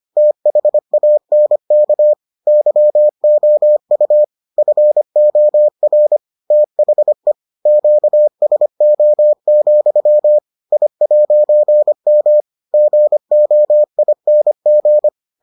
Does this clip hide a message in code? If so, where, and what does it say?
Yes: Morse code from 0.2 to 15.1 s, heard as a keyed tone, about 25 wpm, 600 Hz. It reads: THANK YOU FOR THE QSO, I'M GOING